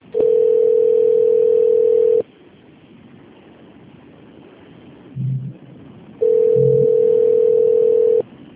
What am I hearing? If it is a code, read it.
Ringback tone (North America), 440 + 480 Hz.